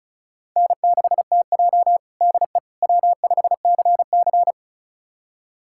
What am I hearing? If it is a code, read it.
Morse code, heard as a keyed tone, about 34 wpm, 700 Hz. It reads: N6TJ DE W5CC
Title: USA Callsign DE Callsign - 4 Characters - 35wpm